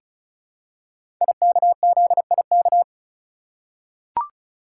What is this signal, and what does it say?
Morse code, heard as a keyed tone, about 36 wpm, 700 Hz. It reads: IKZIK
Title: Code Groups - 5 Letters - 35wpm